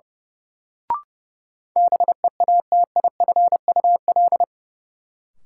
Morse code, heard as a keyed tone, about 30 wpm, 700 Hz. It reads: BEATIFUL
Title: Sentences from Top 1300 Words - New Word Drill - 30wpm